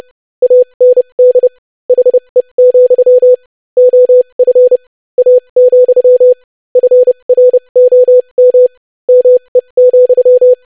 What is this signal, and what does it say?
Morse code, heard as a keyed tone, about 30 wpm, spaced as for 25 wpm, 500 Hz. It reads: AND HE, OF A, FROM ME,
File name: lesson / basicphrase-25wpm